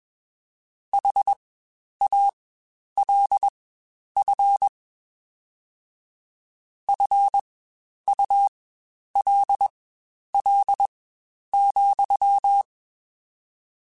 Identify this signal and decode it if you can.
Morse code, heard as a keyed tone, about 21 wpm, spaced as for 10 wpm, 800 Hz. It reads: HALF FULL,
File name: Spaced / Chuckles Spaced-mp3s / HalfFull sp @20